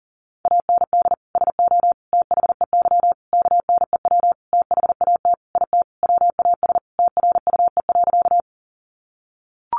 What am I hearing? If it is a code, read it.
Morse code, heard as a keyed tone, about 40 wpm, 700 Hz. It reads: AND SO THEY KNEW THAT IT WAS TRUE.
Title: Sentences from Top 400 Words 40wpm